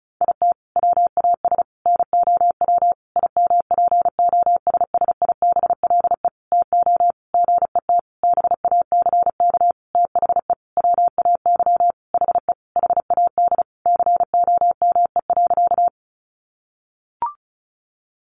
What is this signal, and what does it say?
Morse code, heard as a keyed tone, about 34 wpm, 700 Hz. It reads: IT WAS NOW IMPOSSIBLE TO GET BACK THE WAY HE HAD COME.
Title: Sentences from Top 1100 Words 35wpm